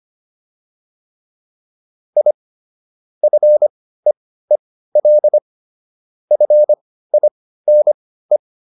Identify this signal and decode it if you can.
Morse code, heard as a keyed tone, about 25 wpm, spaced as for 15 wpm, 600 Hz. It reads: I FEEL FINE
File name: PR305 15